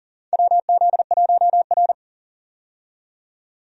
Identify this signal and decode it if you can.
Morse code, heard as a keyed tone, about 40 wpm, 700 Hz. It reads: WZ1R
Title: USA Call Signs - 4 Characters - 40wpm